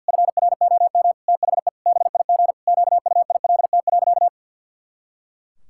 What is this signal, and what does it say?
Morse code, heard as a keyed tone, about 50 wpm, 700 Hz. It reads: FROM THE BIG QUILT.
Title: Pangram Challenge v2 - 50wpm